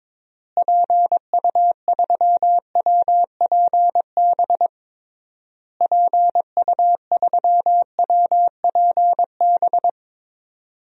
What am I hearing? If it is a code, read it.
Morse code, heard as a keyed tone, about 22 wpm, 700 Hz. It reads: PU3WPB PU3WPB
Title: DXCC-100 Call Signs - 6 Characters - Repeated 3x - 22wpm